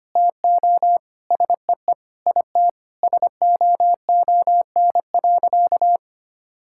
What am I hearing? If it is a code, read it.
Morse code, heard as a keyed tone, about 25 wpm, 700 Hz. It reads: TO SEE IT SOON.
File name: practice / 25wpm / Sentences with 4-Word Phrases - 25wpm